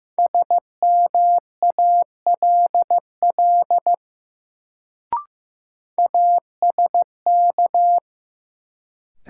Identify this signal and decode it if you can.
Morse code, heard as a keyed tone, about 15 wpm, 700 Hz. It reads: SMALL ASK
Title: Sentences from Top 200 Words - New Word Drill - 15wpm